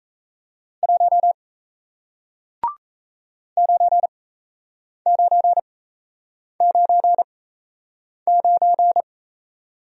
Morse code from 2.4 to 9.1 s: 9999

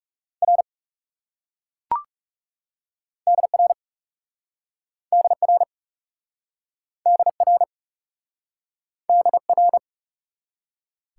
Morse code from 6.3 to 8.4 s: DR

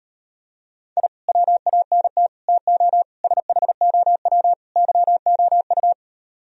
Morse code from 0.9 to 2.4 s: I WANT

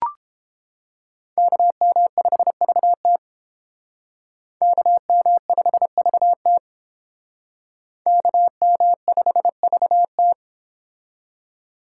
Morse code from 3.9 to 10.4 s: KM5VT KM5VT

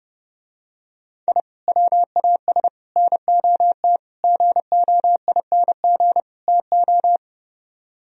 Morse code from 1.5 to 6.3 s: WAS NOT GOING

